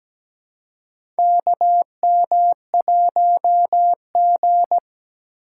Morse code, heard as a keyed tone, about 17 wpm, 700 Hz. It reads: KM1G
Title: USA Call Signs - 2x Character Spacing - 17wpm